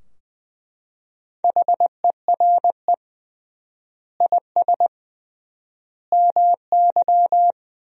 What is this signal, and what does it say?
Morse code, heard as a keyed tone, about 20 wpm, 700 Hz. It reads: HERE IS MY
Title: 3x Word-Spacing - Sentences from Top 200 Words - 20wpm